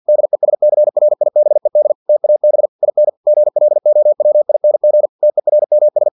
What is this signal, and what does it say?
Morse code, heard as a keyed tone, about 50 wpm, 600 Hz. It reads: DESCRIBED TAB IN GLOWING TERMS
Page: Never 49wpm